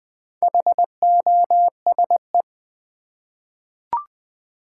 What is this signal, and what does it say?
Morse code, heard as a keyed tone, about 20 wpm, 690 Hz. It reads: HOSE
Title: Easy Words with Hints - 20wpm